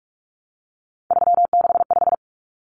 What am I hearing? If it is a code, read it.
Morse code, heard as a keyed tone, about 44 wpm, 700 Hz. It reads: 365
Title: Common Cut Numbers - 3-Digit 45wpm